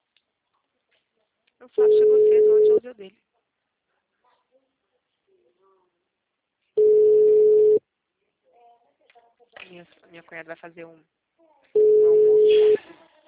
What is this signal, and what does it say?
Ringback tone (Europe, 425 Hz), with speech.
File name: out- -2010- - - .10